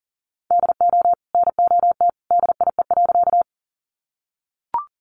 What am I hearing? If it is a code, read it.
Morse code, heard as a keyed tone, about 40 wpm, 700 Hz. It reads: DO NOT DIE.
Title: Sentences from Top 1300 Words 40wpm